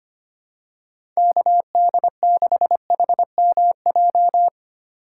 Morse code, heard as a keyed tone, about 25 wpm, 700 Hz. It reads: KD6HMJ
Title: USA Call Signs - 6 Characters - 2x Character Spacing - 25wpm